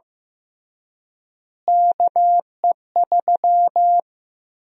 Morse code, heard as a keyed tone, about 15 wpm, 700 Hz. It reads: KE3
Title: USA Call Sign Prefixes - Repeated 2x - 15wpm